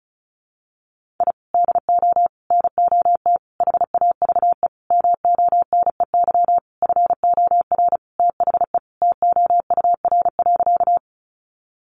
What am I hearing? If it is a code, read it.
Morse code, heard as a keyed tone, about 34 wpm, 700 Hz. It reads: I DO NOT HAVE MONEY FOR THE TOUR.
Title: Sentences from Top 1400 Words 35wpm